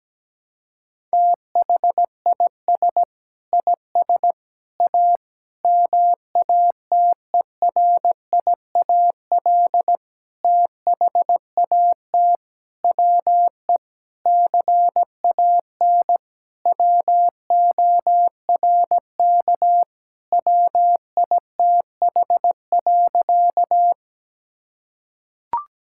Morse code, heard as a keyed tone, about 17 wpm, 700 Hz. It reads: THIS IS A MATERIAL THAT WE CAN WORK WITH.
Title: Sentences from Top 1000 Words - Review Part 5 - 17wpm